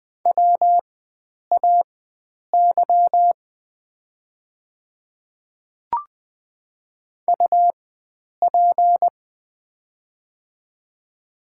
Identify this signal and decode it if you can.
Morse code, heard as a keyed tone, about 20 wpm, spaced as for 9 wpm, 700 Hz. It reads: WAY UP
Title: 4x Character Spacing - Top 100 Words - 20wpm